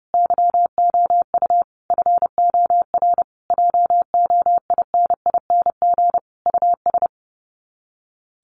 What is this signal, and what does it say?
Morse code, heard as a keyed tone, about 30 wpm, 700 Hz. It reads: YOU FOR JOINING US
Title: Sets of 5 Words 30wpm